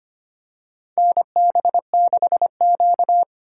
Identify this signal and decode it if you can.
Morse code, heard as a keyed tone, about 25 wpm, 700 Hz. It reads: NB6Q
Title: USA Call Signs - 4 Characters - 2x Character Spacing - 25wpm